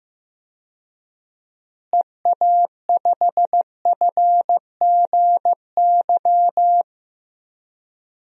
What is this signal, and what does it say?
Morse code, heard as a keyed tone, about 15 wpm, 700 Hz. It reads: EA5FGY
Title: DXCC-100 Call Signs - 6 Characters - Repeated 2x - 15wpm